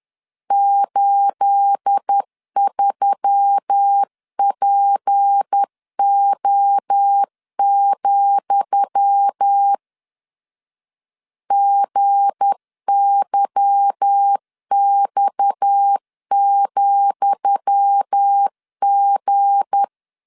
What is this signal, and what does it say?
Morse code, heard as a keyed tone, about 10 wpm, 800 Hz. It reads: 83PO, GYX,G
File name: LEZIONI / 10 SIGNS / 10D COMA 10wpm 3013